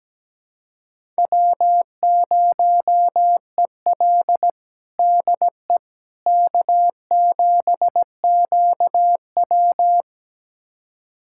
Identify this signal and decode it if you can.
Morse code, heard as a keyed tone, about 17 wpm, 700 Hz. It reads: W0EL DE K7QW
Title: USA Callsign DE Callsign - 4 Characters - 17wpm